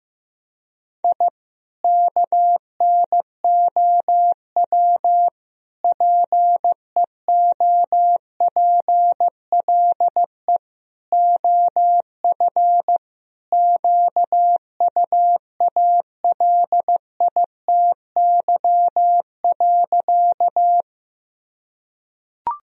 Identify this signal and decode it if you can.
Morse code, heard as a keyed tone, about 15 wpm, 700 Hz. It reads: I KNOW PEOPLE OF QUALITY.
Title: Sentences from Top 1000 Words - Review Part 4 - 15wpm